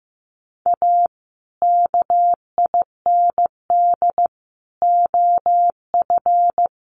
Morse code, heard as a keyed tone, about 15 wpm, 700 Hz. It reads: A KIND OF
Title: Sets of 3 Words 15wpm